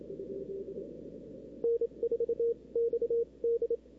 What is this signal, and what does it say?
Morse code, heard as a keyed tone, about 27 wpm, spaced as for 23 wpm, 470 Hz. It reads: N4XD
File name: CW / 2005-k5tr arrl cw dx -03:00